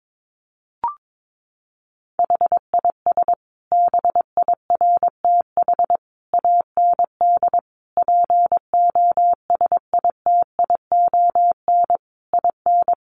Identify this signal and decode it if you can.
Morse code, heard as a keyed tone, about 22 wpm, 700 Hz. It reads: HIS BIRTH AND POSITION IN
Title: Sentences from Top 900 Words 22wpm